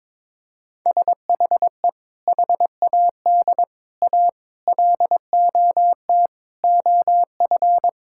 Morse code, heard as a keyed tone, about 22 wpm, 700 Hz. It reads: SHE HAD A LOT OF
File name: practice / 22wpm / Sentences with 4-Word Phrases - 22wpm